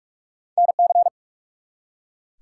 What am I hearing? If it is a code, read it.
Morse code, heard as a keyed tone, about 44 wpm, 700 Hz. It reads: NC
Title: US States and Canadian Provinces 45wpm